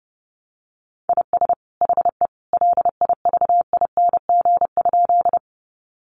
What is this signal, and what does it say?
Morse code, heard as a keyed tone, about 30 wpm, 700 Hz. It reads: IS HE LIVING?
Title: Sentences from Top 1100 Words 30wpm